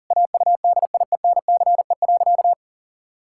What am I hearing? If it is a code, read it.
Morse code, heard as a keyed tone, about 40 wpm, 700 Hz. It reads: AUDIENCE.
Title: Sentences from Top 1000 Words - Review Part 5 - 40wpm